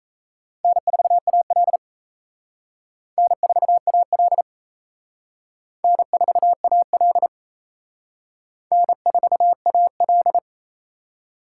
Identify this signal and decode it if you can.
Morse code, heard as a keyed tone, about 34 wpm, 700 Hz. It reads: N4AL N4AL N4AL N4AL
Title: Speed Racing - USA 1x2 Call Signs 28wpm